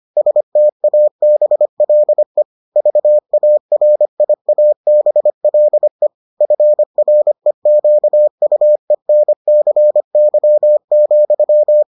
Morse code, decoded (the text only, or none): STABLE VARIABLE FREQUENCY,